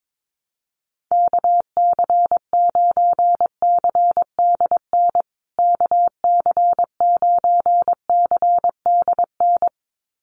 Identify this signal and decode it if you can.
Morse code, heard as a keyed tone, about 22 wpm, 700 Hz. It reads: KC9CDN KC9CDN
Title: USA Call Signs Repeated 2x 22wpm